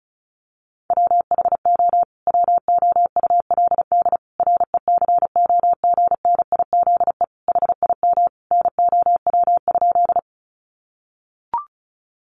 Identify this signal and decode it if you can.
Morse code, heard as a keyed tone, about 34 wpm, 700 Hz. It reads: WHO WOULD RECOGNIZE HIM NOW?
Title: Sentences from Top 1400 Words 35wpm